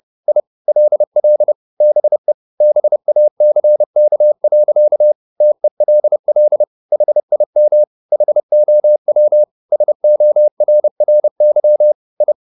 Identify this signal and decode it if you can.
Morse code, heard as a keyed tone, about 30 wpm, 600 Hz. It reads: I LL BE BACK. TELL HIM HOW SORRY I